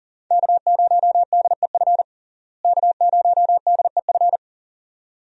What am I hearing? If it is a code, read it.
Morse code, heard as a keyed tone, about 40 wpm, 700 Hz. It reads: K0DEF K0DEF
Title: USA Call Signs - 5 Characters - Repeated 2x - 40wpm